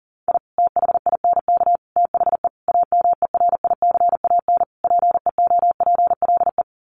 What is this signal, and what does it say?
Morse code, heard as a keyed tone, about 40 wpm, 700 Hz. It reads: I THINK THE AMERICAN PEOPLE
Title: Sets of 5 Words Encore 40wpm